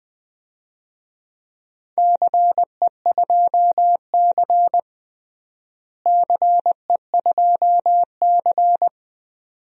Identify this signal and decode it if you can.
Morse code, heard as a keyed tone, about 20 wpm, 700 Hz. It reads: CE2C CE2C
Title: DXCC-100 Call Signs - 4 Characters - Repeated 2x - 20wpm